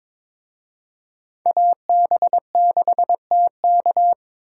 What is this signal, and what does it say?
Morse code, heard as a keyed tone, about 26 wpm, 700 Hz. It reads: AB6TK
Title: Speed Racing - USA 2x2 Call Signs 22wpm